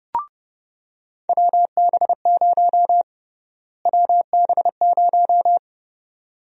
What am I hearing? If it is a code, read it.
Morse code, heard as a keyed tone, about 30 wpm, 700 Hz. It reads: WB0 WB0
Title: USA Call Sign Prefixes - Repeated 2x - 30wpm